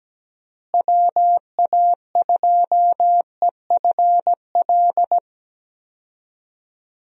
Morse code, heard as a keyed tone, about 17 wpm, 700 Hz. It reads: WA2EFL
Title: USA Call Signs - 6 Characters - Repeated 3x - 17wpm